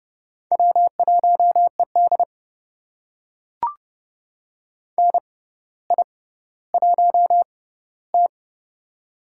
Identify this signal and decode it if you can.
Morse code, heard as a keyed tone, about 30 wpm, spaced as for 15 wpm, 700 Hz. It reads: W1ED NI1T
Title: USA Call Signs - 4 Characters - 6x Character Spacing - 30wpm